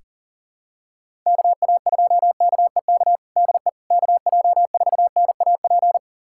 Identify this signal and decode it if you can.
Morse code, heard as a keyed tone, about 40 wpm, 700 Hz. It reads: KA2KEK DE KJ4NAP
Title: USA Callsign DE Callsign - 6 Characters - 40wpm